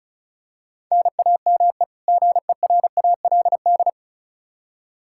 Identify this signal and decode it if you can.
Morse code, heard as a keyed tone, about 36 wpm, 700 Hz. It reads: NAME GERALD
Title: Practice QSO - Exchange 1 - Beginning - 35wpm